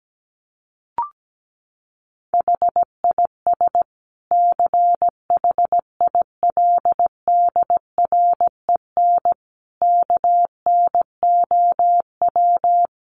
Morse code, heard as a keyed tone, about 17 wpm, 700 Hz. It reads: HIS CHILDREN KNOW
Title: Sentences from Top 900 Words 17wpm